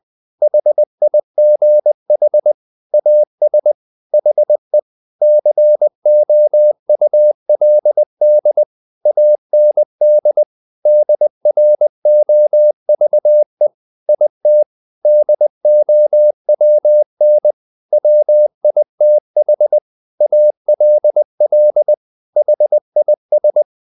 Morse code, heard as a keyed tone, about 20 wpm, 600 Hz. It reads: HIGH AS HE COULD AND DROVE IT DOWN WITH ALL HIS
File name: the old man and the sea part 2 20wpm